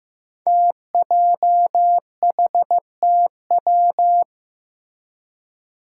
Morse code, heard as a keyed tone, about 15 wpm, 700 Hz. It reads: TJHTW
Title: Five Challenging Letter-Number Combinations 15wpm